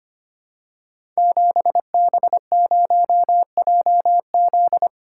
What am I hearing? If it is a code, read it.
Morse code, heard as a keyed tone, about 25 wpm, 700 Hz. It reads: 7B0JZ
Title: Code Groups - 5 Letter:Numbers - 25wpm